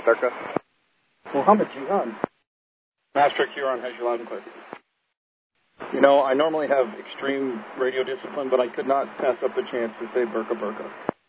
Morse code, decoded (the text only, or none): none